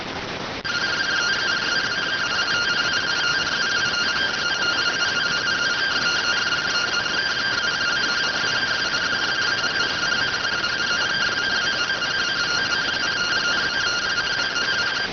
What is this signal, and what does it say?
Two-tone teleprinter signal, about 170 Hz shift.